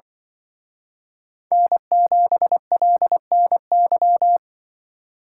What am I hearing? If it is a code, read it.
Morse code, heard as a keyed tone, about 24 wpm, 700 Hz. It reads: N7LNY
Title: Speed Racing - USA 1x3 Call Signs 20wpm